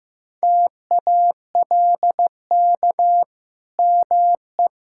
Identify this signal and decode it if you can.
Morse code, heard as a keyed tone, about 14 wpm, 700 Hz. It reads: TALK ME
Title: Sentences from Top 1000 Words - Review Part 5 - 15wpm